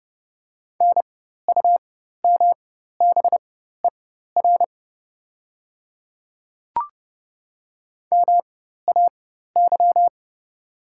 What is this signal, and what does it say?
Morse code, heard as a keyed tone, about 31 wpm, spaced as for 14 wpm, 700 Hz. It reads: NUMBER MAY
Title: 4x Character Spacing - Top 100 Words - 30wpm